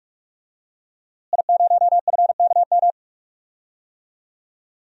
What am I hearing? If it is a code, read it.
Morse code, heard as a keyed tone, about 44 wpm, 700 Hz. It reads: I0FKM
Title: DXCC-100 Call Signs - 5 Characters - Repeated 3x - 45wpm